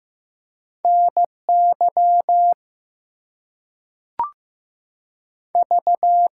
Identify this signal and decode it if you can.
Morse code, heard as a keyed tone, about 15 wpm, 700 Hz. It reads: NY V
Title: US States and Canadian Provinces 15wpm